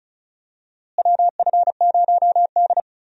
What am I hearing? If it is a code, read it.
Morse code, heard as a keyed tone, about 36 wpm, 700 Hz. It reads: WF0D
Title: USA Call Signs - 4 Characters - Repeated 3x - 35wpm